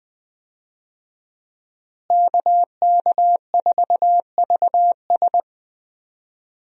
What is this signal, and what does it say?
Morse code, heard as a keyed tone, about 20 wpm, 700 Hz. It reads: KK4VS
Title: USA Call Signs - 5 Characters - Repeated 3x - 20wpm